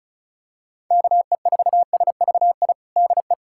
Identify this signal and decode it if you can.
Morse code, heard as a keyed tone, about 34 wpm, 700 Hz. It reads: KE4SVI DE
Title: USA Callsign DE Callsign - 6 Characters - 35wpm